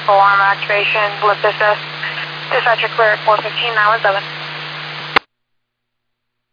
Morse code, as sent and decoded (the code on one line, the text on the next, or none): none